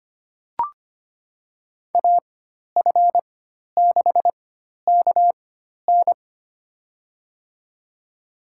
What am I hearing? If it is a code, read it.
Morse code, heard as a keyed tone, about 25 wpm, spaced as for 12 wpm, 700 Hz. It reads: AF6KN
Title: USA Call Signs - 5 Characters - 4x Character Spacing - 25wpm